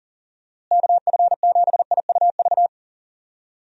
Morse code, heard as a keyed tone, about 40 wpm, 700 Hz. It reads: KF7IUV
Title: USA Call Signs - 6 Characters - Repeated 2x - 40wpm